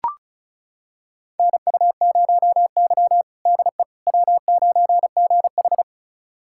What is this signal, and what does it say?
Morse code, heard as a keyed tone, about 36 wpm, 700 Hz. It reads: NU0Y DE W9GH
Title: USA Callsign DE Callsign - 4 Characters - 35wpm